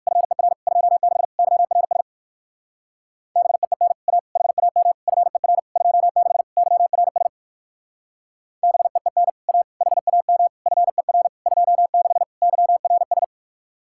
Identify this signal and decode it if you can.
Morse code, heard as a keyed tone, about 58 wpm, 700 Hz. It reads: FER 26 YRS BEEN A HAM FER 26 YRS BEEN A HAM FER 26 YRS